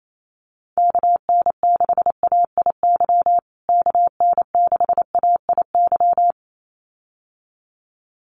Morse code, heard as a keyed tone, about 28 wpm, 700 Hz. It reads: KN6AIY KN6AIY
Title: USA Call Signs Repeated 2x 28wpm